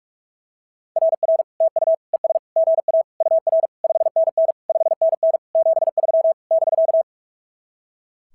Morse code, heard as a keyed tone, about 42 wpm, 660 Hz. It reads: RR TU ES GA UR 5NN 5NN 73 <BK>
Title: SOTA QSO Practice 45wpm